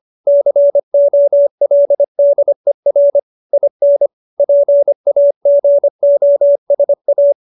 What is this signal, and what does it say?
Morse code, heard as a keyed tone, about 25 wpm, 570 Hz. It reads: COLDER IN PAGOSA
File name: QSO312 25